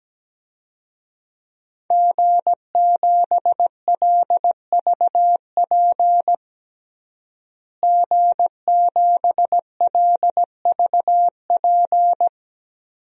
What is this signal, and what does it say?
Morse code, heard as a keyed tone, about 17 wpm, 700 Hz. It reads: G7LVP G7LVP
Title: DXCC-100 Call Signs - 5 Characters - Repeated 2x - 17wpm